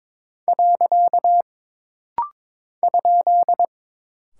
Morse code, heard as a keyed tone, about 22 wpm, 700 Hz. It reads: . ?
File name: practice / 22wpm / Single Punctuation-Prosign - Rapid-Fire - 22wpm